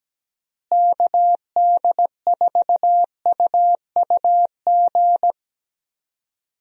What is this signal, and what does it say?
Morse code, heard as a keyed tone, about 17 wpm, 700 Hz. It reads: KD4UUG
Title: USA Call Signs - 6 Characters - Repeated 2x - 17wpm